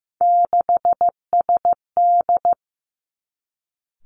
Morse code, heard as a keyed tone, about 15 wpm, 700 Hz. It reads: 6SD
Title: Three Challenging Letter-Number Combinations 15wpm